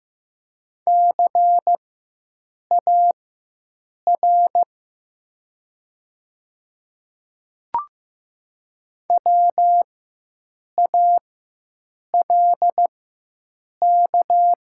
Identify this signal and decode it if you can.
Morse code, heard as a keyed tone, about 15 wpm, spaced as for 7 wpm, 700 Hz. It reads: CAR WALK
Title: 4x Character Spacing - Top 300 Words - 15wpm